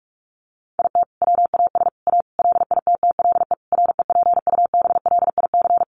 Morse code, heard as a keyed tone, about 44 wpm, 700 Hz. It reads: IT WAS A LITTLE REPUBLIC
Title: Sentences from Top 1500 Words 45wpm